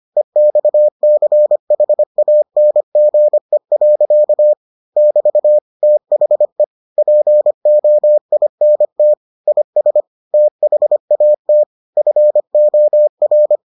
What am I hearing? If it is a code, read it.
Morse code, heard as a keyed tone, about 25 wpm, 600 Hz. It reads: EXCHANGE. <BT> THE POINT IS THAT FOR